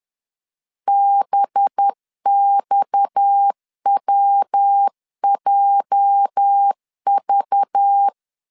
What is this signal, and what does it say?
Morse code, heard as a keyed tone, about 10 wpm, 800 Hz. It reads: BXWJV
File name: LEZIONI / 8 YJ45 / 8 YJ 10wpm 2651